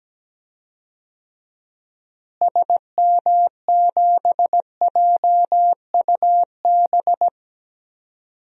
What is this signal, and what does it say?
Morse code, heard as a keyed tone, about 17 wpm, 700 Hz. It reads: SM7JUB
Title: DXCC-100 Call Signs - 6 Characters - Repeated 2x - 17wpm